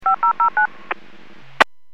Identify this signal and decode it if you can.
DTMF, heard as telephone touch tones, tones of about 110 ms. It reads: 5009
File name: WPS 451.25